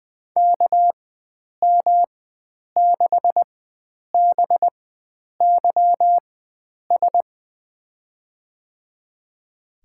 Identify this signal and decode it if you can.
Morse code, heard as a keyed tone, about 20 wpm, spaced as for 9 wpm, 700 Hz. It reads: KM6BYS
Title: USA Call Signs - 6 Characters - 4x Character Spacing - 20wpm